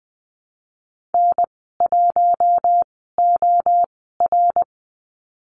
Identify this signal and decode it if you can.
Morse code, heard as a keyed tone, about 20 wpm, spaced as for 14 wpm, 700 Hz. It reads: N1OR
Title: 069 Lesson 12g - Letter d - Call Signs 20wpm